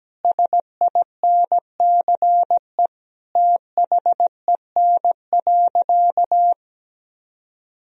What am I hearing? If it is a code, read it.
Morse code, heard as a keyed tone, about 17 wpm, 700 Hz. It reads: SINCE THEN.
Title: Sentences from Top 1000 Words - Review Part 1 - 17wpm